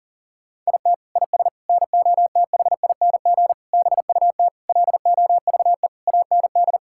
Morse code, heard as a keyed tone, about 40 wpm, 700 Hz. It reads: IT IS NOTHING BUT LOVE AND
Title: Sentences from Top 1600 Words 40wpm